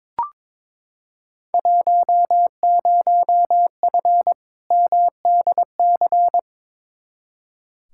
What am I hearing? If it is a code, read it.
Morse code, heard as a keyed tone, about 22 wpm, 700 Hz. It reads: 10F MDC
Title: Field Day Exchanges Encore - 22wpm